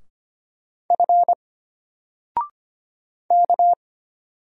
Morse code, heard as a keyed tone, about 25 wpm, 700 Hz. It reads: F K
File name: Single-Letter-Number-25wpm